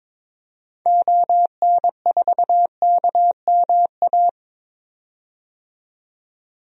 Morse code, heard as a keyed tone, about 21 wpm, 700 Hz. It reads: ON4KMA
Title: DXCC-100 Call Signs - 6 Characters - 22wpm